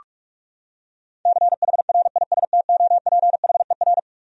Morse code, heard as a keyed tone, about 44 wpm, 700 Hz. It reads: CHRISTOPHER
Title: USA Common Names - 45wpm